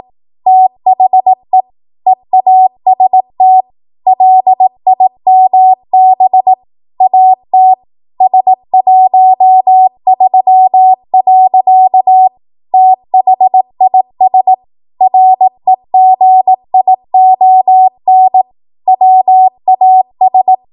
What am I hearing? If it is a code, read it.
Morse code, heard as a keyed tone, about 18 wpm, 750 Hz. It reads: THE EAST LIMB AT S13. THIS REGION WAS
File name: Morse / 18WPM